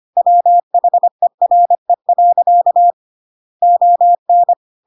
Morse code, heard as a keyed tone, about 25 wpm, 700 Hz. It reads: WHERE. ON
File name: Pinocchio / Pinocchio ch2 25 25 ik1hge